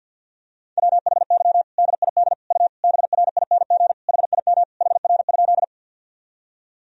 Morse code, heard as a keyed tone, about 46 wpm, 700 Hz. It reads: WHY DID U BRING HIM HR?
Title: Sentences from Top 500 Words and Abbreviations 50wpm